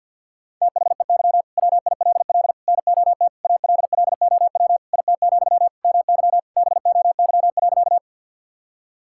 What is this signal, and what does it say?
Morse code, heard as a keyed tone, about 53 wpm, 700 Hz. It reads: THEY WILL NOT ALLOW IT, MY BOY.
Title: Sentences from Top 1000 Words - Review Part 5 - 50wpm